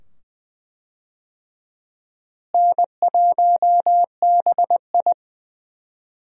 Morse code, heard as a keyed tone, about 20 wpm, 700 Hz. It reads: N1BI
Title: USA Call Signs - 4 Characters - Repeated 2x - 20wpm